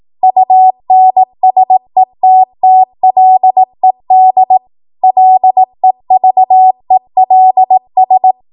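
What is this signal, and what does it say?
Morse code, heard as a keyed tone, about 18 wpm, 750 Hz. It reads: UNSETTLED LEVELS
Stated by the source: ARRL 18 wpm Morse code practice archive transmission.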